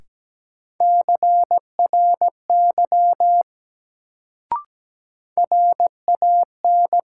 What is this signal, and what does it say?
Morse code, heard as a keyed tone, about 17 wpm, 700 Hz. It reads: CRY RAN
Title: Sentences from Top 500 Words - New Word Drill - 17wpm